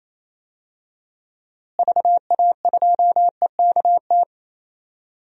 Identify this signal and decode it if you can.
Morse code, heard as a keyed tone, about 28 wpm, 700 Hz. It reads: VA2EKT
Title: DXCC-100 Call Signs - 6 Characters - Repeated 2x - 28wpm